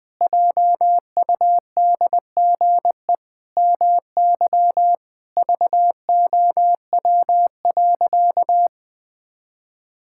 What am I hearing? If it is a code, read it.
Morse code, heard as a keyed tone, about 20 wpm, 700 Hz. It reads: JUDGE MY VOW.
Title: Pangram Challenge v2 - 20wpm